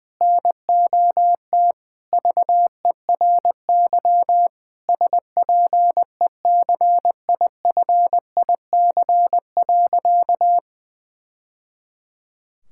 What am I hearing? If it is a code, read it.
Morse code, heard as a keyed tone, about 21 wpm, 700 Hz. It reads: NOT VERY SPECIFIC.
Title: Sentences from Top 1000 Words - Review Part 6 - 20wpm